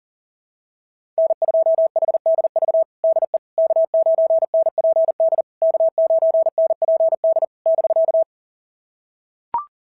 Morse code, heard as a keyed tone, about 40 wpm, 690 Hz. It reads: N2HDV DE K9NPD K9NPD <BK>
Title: SOTA QSO Practice 40wpm